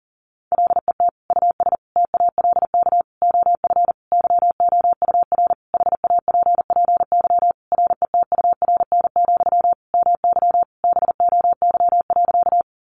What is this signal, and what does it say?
Morse code, heard as a keyed tone, about 37 wpm, 700 Hz. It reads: LET US TALK OF YOUR HAPPY RETURN, MY BOY.
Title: Sentences from Top 1000 Words 40wpm